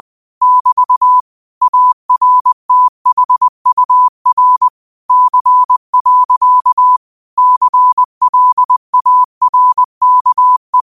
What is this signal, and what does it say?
Morse code, heard as a keyed tone, about 20 wpm, 1000 Hz. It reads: <BT> ARTHUR C. CLARKE